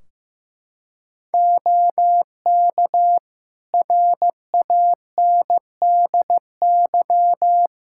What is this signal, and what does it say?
Morse code, heard as a keyed tone, about 15 wpm, 700 Hz. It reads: OK RANDY
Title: Practice QSO - Exchange 4 - Ending - 15wpm